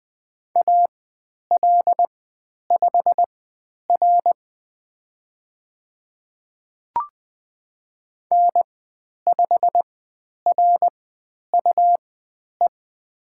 Morse code, heard as a keyed tone, about 20 wpm, spaced as for 10 wpm, 700 Hz. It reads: AL5R N5RUE
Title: 102 Lesson 17g - Letter u - Call Signs 20wpm